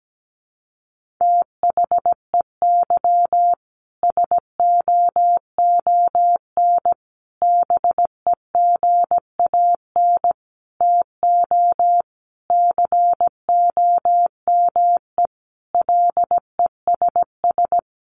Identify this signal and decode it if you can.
Morse code, heard as a keyed tone, about 17 wpm, 700 Hz. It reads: THEY SOON BEGAN TO COME LESS
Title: Sentences from Top 1300 Words 17wpm